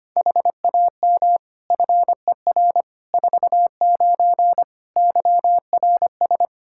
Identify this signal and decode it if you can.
Morse code, heard as a keyed tone, about 25 wpm, 700 Hz. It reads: HAM FER 49 YRS
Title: Practice QSO - Exchange 3 - More Info - 25wpm